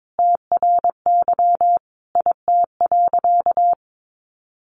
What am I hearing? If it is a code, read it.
Morse code, heard as a keyed tone, about 22 wpm, 700 Hz. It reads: TRY IT.
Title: Sentences from Top 1100 Words 22wpm